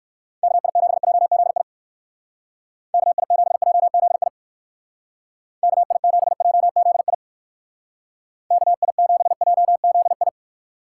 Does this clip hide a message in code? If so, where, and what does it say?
Yes: Morse code from 0.4 to 10.3 s, heard as a keyed tone, about 59 wpm, 700 Hz. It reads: KI7JZI KI7JZI KI7JZI KI7JZI